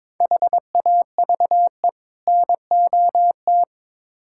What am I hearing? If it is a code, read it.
Morse code, heard as a keyed tone, about 21 wpm, 700 Hz. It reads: HAVE NOT
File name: practice / 22wpm / Build A Phrase - 5-Word Phrases - 22wpm